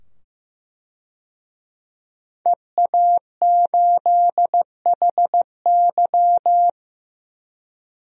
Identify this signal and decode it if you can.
Morse code, heard as a keyed tone, about 15 wpm, 700 Hz. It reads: EA8HY
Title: DXCC-100 Call Signs - 5 Characters - 15wpm